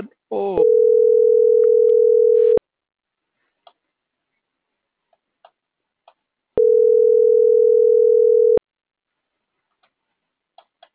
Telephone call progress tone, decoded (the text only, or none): ringback tone